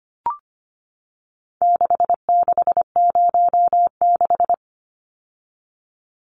Morse code, heard as a keyed tone, about 25 wpm, 700 Hz. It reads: 6606
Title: Numbers - 4-Digit 25wpm